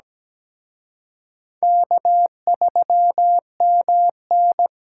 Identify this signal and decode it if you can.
Morse code, heard as a keyed tone, about 20 wpm, 700 Hz. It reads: K3MN
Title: Speed Racing - USA 1x2 Call Signs 17wpm